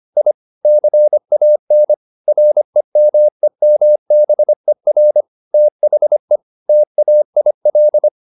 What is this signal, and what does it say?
Morse code, heard as a keyed tone, about 25 wpm, 600 Hz. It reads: I CAN REMEMBER THE TAIL
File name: the old man and the sea part 1 25wpm